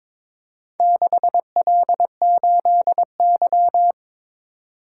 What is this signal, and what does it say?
Morse code, heard as a keyed tone, about 22 wpm, 700 Hz. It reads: 6L8Y
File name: practice / 22wpm / Four Letter-Number Combinations - 22wpm